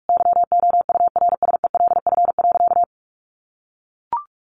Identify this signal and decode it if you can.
Morse code, heard as a keyed tone, about 44 wpm, 700 Hz. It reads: YOURSELF.
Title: Sentences from Top 1100 Words 45wpm